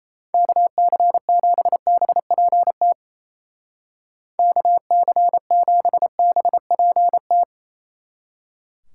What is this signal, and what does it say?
Morse code, heard as a keyed tone, about 34 wpm, 700 Hz. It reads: KC7BPT KC7BPT
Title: Speed Racing - USA Call Signs 28wpm